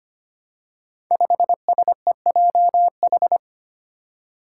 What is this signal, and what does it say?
Morse code, heard as a keyed tone, about 25 wpm, 700 Hz. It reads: 5SEJH
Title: Five Challenging Letter-Number Combinations 25wpm